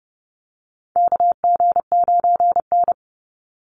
Morse code, heard as a keyed tone, about 30 wpm, 700 Hz. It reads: KG9N
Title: Call Signs Repeated 3x - 30wpm